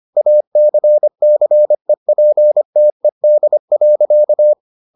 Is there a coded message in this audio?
Yes: Morse code from 0.2 to 4.5 s, heard as a keyed tone, about 25 wpm, 600 Hz. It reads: ACCEPTED.